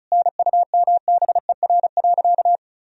Morse code, heard as a keyed tone, about 34 wpm, 700 Hz. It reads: NUMBER.
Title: Sentences from Top 1000 Words - Review Part 1 - 35wpm